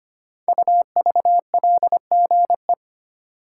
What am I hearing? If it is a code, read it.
Morse code, heard as a keyed tone, about 25 wpm, 700 Hz. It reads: UVLGE
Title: Code Groups - 5 Letter:Numbers - 25wpm